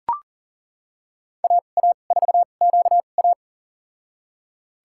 Morse code, heard as a keyed tone, about 38 wpm, spaced as for 28 wpm, 700 Hz. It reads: AA4QA
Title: USA Call Signs - 2x Character Spacing - 40wpm